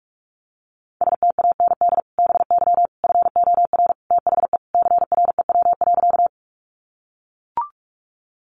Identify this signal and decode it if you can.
Morse code, heard as a keyed tone, about 44 wpm, 700 Hz. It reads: STAND BY FOR THE CREW.
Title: Sentences from Top 1500 Words 45wpm